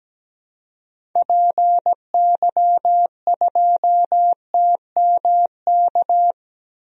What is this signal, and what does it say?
Morse code, heard as a keyed tone, about 17 wpm, 700 Hz. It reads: PY2TMK
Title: DXCC-100 Call Signs - 6 Characters - 17wpm